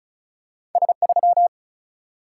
Morse code, heard as a keyed tone, about 35 wpm, 700 Hz. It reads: S 3
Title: Single-Letter Single-Number 35wpm